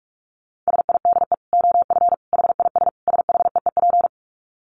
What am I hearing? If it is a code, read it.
Morse code, heard as a keyed tone, about 44 wpm, 700 Hz. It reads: SIDE OF HIS SHEEP
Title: Dit Challenge 45wpm